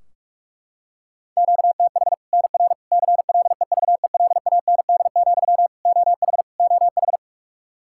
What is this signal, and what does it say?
Morse code, heard as a keyed tone, about 41 wpm, 700 Hz. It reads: QTH NR CLEVELAND, OH OH
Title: K1USN SST QSO Practice 45wpm